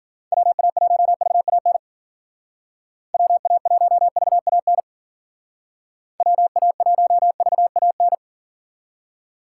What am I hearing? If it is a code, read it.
Morse code, heard as a keyed tone, about 50 wpm, 700 Hz. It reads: WA1VAN WA1VAN WA1VAN
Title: Speed Racing - USA 2x3 Call Signs 40wpm